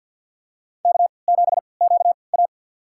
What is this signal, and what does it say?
Morse code, heard as a keyed tone, about 51 wpm, spaced as for 28 wpm, 700 Hz. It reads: K7QA
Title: USA Call Signs - 4 Characters - 3x Character Spacing - 50wpm